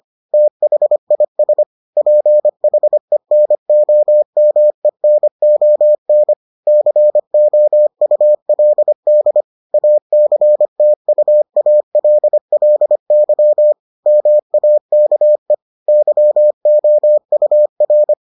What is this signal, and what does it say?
Morse code, heard as a keyed tone, about 25 wpm, 600 Hz. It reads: THIS PHENOMENON COULD ACTUALLY MAKE YOUR